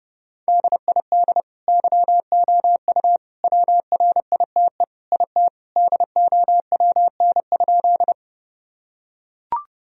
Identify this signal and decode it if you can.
Morse code, heard as a keyed tone, about 30 wpm, 700 Hz. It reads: DID YOU WRITE IT DOWN?
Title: Sentences from Top 1000 Words - Review Part 1 - 30wpm